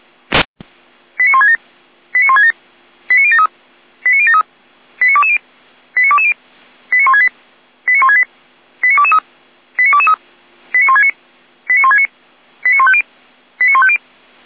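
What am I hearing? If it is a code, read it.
Five-tone selective call (ZVEI), heard as quick runs of five tones, tones of about 70 ms. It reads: 89167 89167 89073 89073 89220 89220 89168 89168 89223 89223 89169 89169 89150 89150
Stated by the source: ZVEI five-tone call received by SDR.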